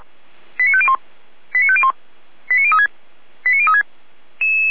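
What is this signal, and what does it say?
Five-tone selective call (ZVEI), heard as quick runs of five tones, tones of about 70 ms. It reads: 89501 89501 89036 89036
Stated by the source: ZVEI five-tone call received on SDR, from fire brigade Feuerwehr Orlamünde (f85360).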